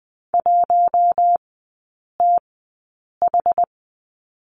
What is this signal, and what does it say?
Morse code, heard as a keyed tone, about 20 wpm, spaced as for 8 wpm, 700 Hz. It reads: 1TH
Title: 060 Lesson 11e - Letter h - Three Characters 20wpm